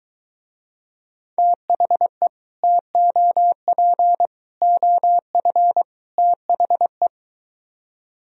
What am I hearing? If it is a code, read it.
Morse code, heard as a keyed tone, about 23 wpm, 700 Hz. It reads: THE TOP OF THE